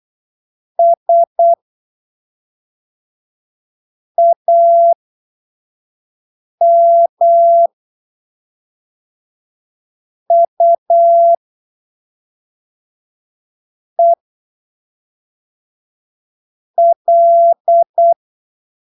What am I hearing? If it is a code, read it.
Morse code, heard as a keyed tone, about 8 wpm, spaced as for 3 wpm, 680 Hz. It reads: SAMUEL